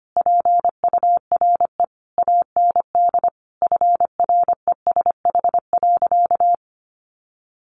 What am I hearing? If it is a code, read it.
Morse code, heard as a keyed tone, about 25 wpm, 700 Hz. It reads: PURE AND FRESH.
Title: Sentences from Top 1500 Words 25wpm